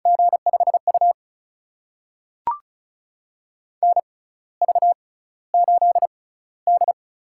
Morse code, heard as a keyed tone, about 36 wpm, spaced as for 16 wpm, 700 Hz. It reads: G5U NV8D
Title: USA Call Signs - 4 Characters - 6x Character Spacing - 35wpm